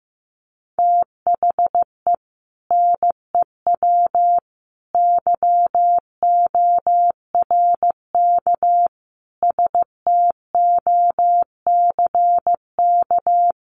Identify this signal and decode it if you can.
Morse code, heard as a keyed tone, about 15 wpm, 700 Hz. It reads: THE NEW YORK STOCK
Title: Sets of 5 Words 15wpm